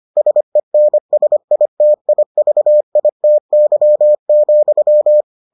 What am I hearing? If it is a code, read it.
Morse code, heard as a keyed tone, about 25 wpm, 600 Hz. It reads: SENSITIVITY,